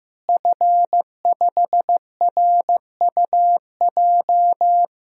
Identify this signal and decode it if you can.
Morse code, heard as a keyed tone, about 15 wpm, 700 Hz. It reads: F5RUJ
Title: DXCC-100 Call Signs - 5 Characters - Repeated 3x - 15wpm